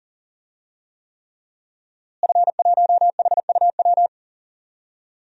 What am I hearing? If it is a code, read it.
Morse code, heard as a keyed tone, about 40 wpm, 700 Hz. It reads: F1HUW
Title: DXCC-100 Call Signs - 5 Characters - Repeated 3x - 40wpm